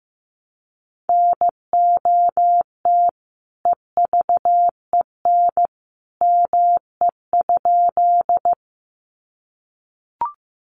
Morse code, heard as a keyed tone, about 15 wpm, 700 Hz. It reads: NOT EVEN ME?
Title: Sentences from Top 600 Words 15wpm